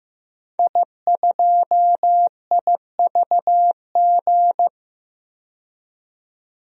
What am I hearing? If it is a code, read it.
Morse code, heard as a keyed tone, about 15 wpm, 700 Hz. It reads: I2IVG
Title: DXCC-100 Call Signs - 5 Characters - 15wpm